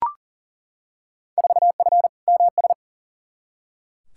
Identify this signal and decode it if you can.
Morse code, heard as a keyed tone, about 40 wpm, 700 Hz. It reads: 4F MS
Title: Field Day Exchanges Encore - 40wpm